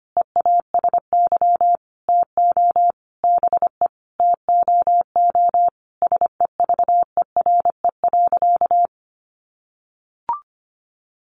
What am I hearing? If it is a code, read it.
Morse code, heard as a keyed tone, about 25 wpm, 700 Hz. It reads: EASY TO BE TOO SEVERE.
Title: Sentences from Top 1400 Words 25wpm